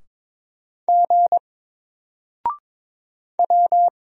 Morse code, heard as a keyed tone, about 22 wpm, 700 Hz. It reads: G W
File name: Single-Letter-Number-22wpm